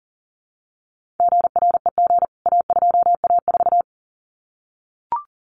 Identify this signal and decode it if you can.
Morse code, heard as a keyed tone, about 39 wpm, 700 Hz. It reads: GREG A2A4
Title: CWT Exchange 40wpm